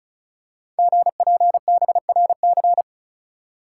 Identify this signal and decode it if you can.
Morse code, heard as a keyed tone, about 36 wpm, 690 Hz. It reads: GPBRC
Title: Code Groups - 5 Letters - 35wpm